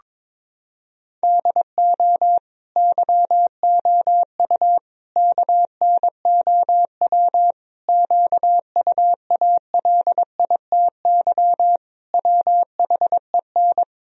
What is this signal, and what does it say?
Morse code, heard as a keyed tone, about 22 wpm, 700 Hz. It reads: DO YOU KNOW QUALITY WHEN
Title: Sentences from Top 1000 Words - Review Part 4 - 22wpm